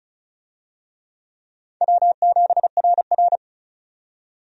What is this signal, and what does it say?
Morse code, heard as a keyed tone, about 36 wpm, 700 Hz. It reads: W7RR
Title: USA Call Signs - 4 Characters - 35wpm